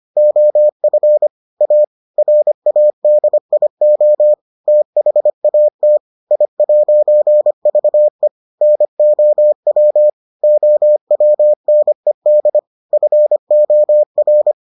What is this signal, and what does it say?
Morse code, heard as a keyed tone, about 25 wpm, 600 Hz. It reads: OF A RADIO THAT I'VE NOW OWNED FOR